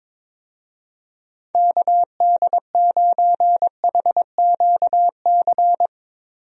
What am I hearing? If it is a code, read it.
Morse code, heard as a keyed tone, about 22 wpm, 700 Hz. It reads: KD9HQC
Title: USA Call Signs - 6 Characters - Repeated 3x - 22wpm